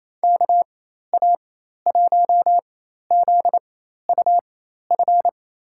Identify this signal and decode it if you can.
Morse code, heard as a keyed tone, about 28 wpm, spaced as for 13 wpm, 700 Hz. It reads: KA1ZUF
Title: USA Call Signs - 6 Characters - 4x Character Spacing - 28wpm